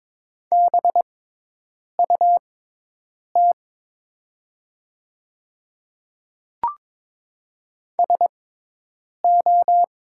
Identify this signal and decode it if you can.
Morse code, heard as a keyed tone, about 22 wpm, spaced as for 8 wpm, 700 Hz. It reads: BUT SO
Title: 6x Character Spacing - Top 100 Words - 22wpm